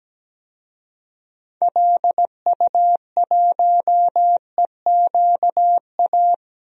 Morse code, heard as a keyed tone, about 17 wpm, 700 Hz. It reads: LU1EQA
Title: DXCC-100 Call Signs - 6 Characters - Repeated 3x - 17wpm